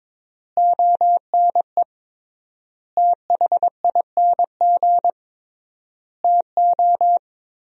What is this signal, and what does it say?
Morse code, heard as a keyed tone, about 21 wpm, 700 Hz. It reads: ONE THING TO
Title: 3x Word-Spacing - Sentences from Top 100 Words - 22wpm